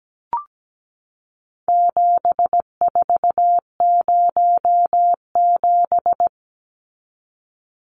Morse code, heard as a keyed tone, about 17 wpm, 700 Hz. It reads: 7407
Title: Numbers - 4-Digit 17wpm